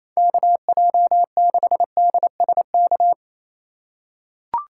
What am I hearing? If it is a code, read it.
Morse code, heard as a keyed tone, about 28 wpm, 700 Hz. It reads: KJ6DSK
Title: USA Call Signs - 2x Character Spacing - 28wpm